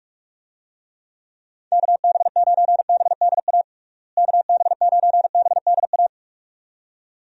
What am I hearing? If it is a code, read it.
Morse code, heard as a keyed tone, about 46 wpm, 700 Hz. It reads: KB9BDA KB9BDA
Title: USA Call Signs - 6 Characters - Repeated 2x - 45wpm